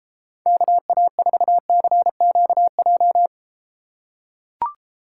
Morse code, heard as a keyed tone, about 32 wpm, 700 Hz. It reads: KA4CQJ